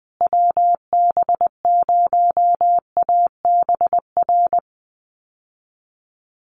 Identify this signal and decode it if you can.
Morse code, heard as a keyed tone, about 20 wpm, 700 Hz. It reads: WB0ABR
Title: USA Call Signs Repeated 3x 20wpm